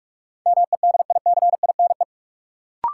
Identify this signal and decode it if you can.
Morse code, heard as a keyed tone, about 46 wpm, 700 Hz. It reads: MEDICINE
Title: Sentences from Top 900 Words - New Word Drill - 45wpm